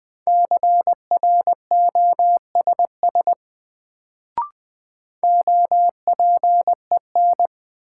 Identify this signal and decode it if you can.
Morse code, heard as a keyed tone, about 20 wpm, 700 Hz. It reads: CROSS OPEN
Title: Sentences from Top 300 Words - New Word Drill - 20wpm